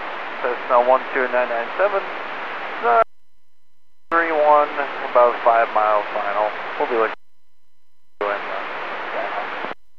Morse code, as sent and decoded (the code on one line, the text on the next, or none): none